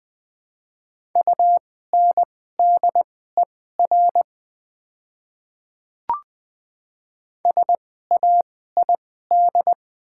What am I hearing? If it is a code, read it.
Morse code, heard as a keyed tone, about 20 wpm, spaced as for 14 wpm, 690 Hz. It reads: UNDER SAID
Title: 2x Character Spacing - Top 300 Words - 20wpm